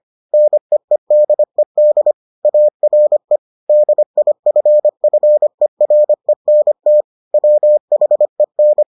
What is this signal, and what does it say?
Morse code, heard as a keyed tone, about 25 wpm, 600 Hz. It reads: NEEDED ARE DIFFERENT WHEN